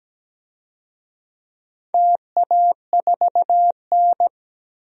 Morse code, heard as a keyed tone, about 17 wpm, 700 Hz. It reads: TA4N
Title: DXCC-100 Call Signs - 4 Characters - 17wpm